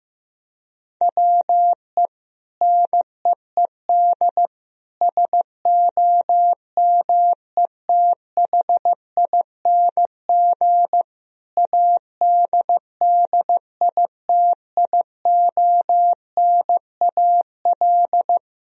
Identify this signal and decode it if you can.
Morse code, heard as a keyed tone, about 15 wpm, 700 Hz. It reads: WE NEED SOMETHING ADDITIONAL
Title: Sentences from Top 1000 Words - Review Part 4 - 15wpm